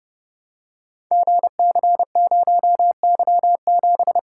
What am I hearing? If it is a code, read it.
Morse code, heard as a keyed tone, about 28 wpm, 700 Hz. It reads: GC0Y7
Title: Code Groups - 5 Letter:Numbers - 30wpm